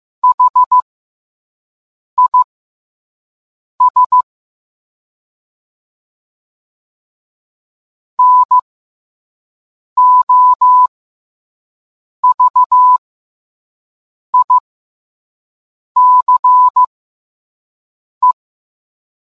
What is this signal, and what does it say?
Morse code, heard as a keyed tone, about 15 wpm, spaced as for 5 wpm, 1000 Hz. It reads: HIS NOVICE